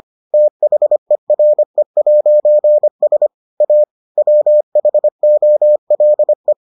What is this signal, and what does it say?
Morse code, heard as a keyed tone, about 25 wpm, 600 Hz. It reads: THERE'S A WHOLE